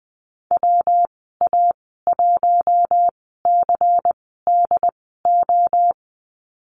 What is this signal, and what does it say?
Morse code, heard as a keyed tone, about 20 wpm, spaced as for 15 wpm, 700 Hz. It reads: WA1CDO
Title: 122 Lesson 20g - Number 3 - Call Signs 20wpm